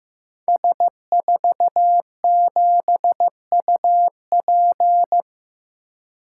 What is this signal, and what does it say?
Morse code, heard as a keyed tone, about 15 wpm, 700 Hz. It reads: S47UP
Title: Code Groups - 5 Letter:Numbers - 15wpm